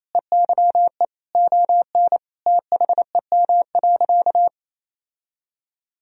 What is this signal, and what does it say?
Morse code, heard as a keyed tone, about 28 wpm, 700 Hz. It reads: EYE ON THEM.
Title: Sentences from Top 1000 Words - Review Part 1 - 28wpm